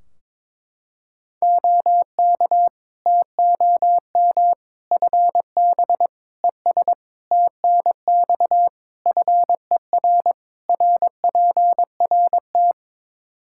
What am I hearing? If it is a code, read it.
Morse code, heard as a keyed tone, about 22 wpm, 700 Hz. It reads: OK TOM FB ES TNX FER RPRT
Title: Practice QSO - Exchange 2 - Info - 22wpm